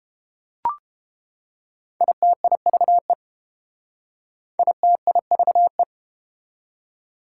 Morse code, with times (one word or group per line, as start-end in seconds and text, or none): 2.0-3.1 ITIVE
4.6-5.8 ITIVE